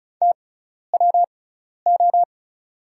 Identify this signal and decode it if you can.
Morse code, heard as a keyed tone, about 34 wpm, spaced as for 12 wpm, 700 Hz. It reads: TWO
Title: 6x Character Spacing - Top 100 Words - 35wpm